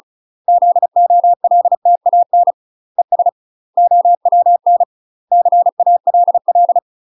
Morse code, heard as a keyed tone, about 33 wpm, 700 Hz. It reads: ZOLTAN ES OWN CALL
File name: UK MASTER QSOs TRAINING / 35WPM / UK MASTER QSOs 0023